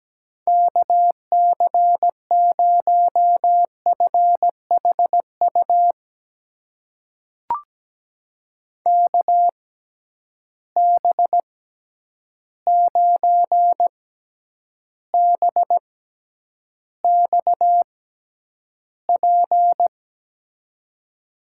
Morse code, heard as a keyed tone, about 17 wpm, spaced as for 9 wpm, 700 Hz. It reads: KC0FHU KB9BXP
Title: USA Call Signs - 6 Characters - 6x Character Spacing - 17wpm